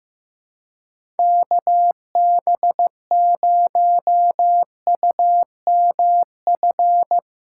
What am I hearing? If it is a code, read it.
Morse code, heard as a keyed tone, about 16 wpm, 700 Hz. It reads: KB0UMF
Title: USA Call Signs - 6 Characters - 2x Character Spacing - 15wpm